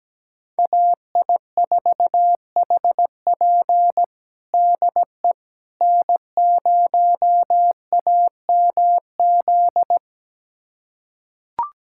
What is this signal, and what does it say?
Morse code, heard as a keyed tone, about 17 wpm, 700 Hz. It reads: AI4HP DE N0AMZ
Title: USA Callsign DE Callsign - 5 Characters - 17wpm